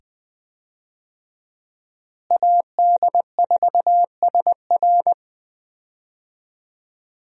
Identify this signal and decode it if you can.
Morse code, heard as a keyed tone, about 20 wpm, 700 Hz. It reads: AD4SR
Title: USA Call Signs - 5 Characters - Repeated 2x - 20wpm